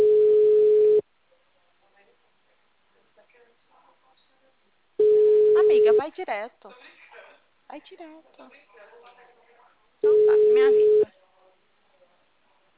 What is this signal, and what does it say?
Ringback tone (Europe, 425 Hz), with speech.
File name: out- -2001- - - .7521